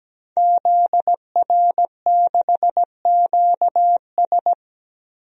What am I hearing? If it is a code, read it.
Morse code, heard as a keyed tone, about 17 wpm, 700 Hz. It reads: ZR6QS
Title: DXCC-100 Call Signs - 5 Characters - 17wpm